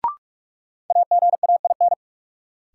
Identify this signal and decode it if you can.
Morse code, heard as a keyed tone, about 43 wpm, 700 Hz. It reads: AGAIN
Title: Sets of 4 Words - New Word Drill - 45wpm